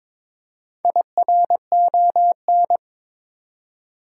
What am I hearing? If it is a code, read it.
Morse code, heard as a keyed tone, about 22 wpm, 700 Hz. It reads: IRON
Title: Four Letter Words - 22wpm